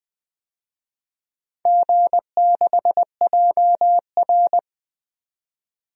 Morse code, heard as a keyed tone, about 20 wpm, 700 Hz. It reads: G6JR
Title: DXCC-100 Call Signs - 4 Characters - 20wpm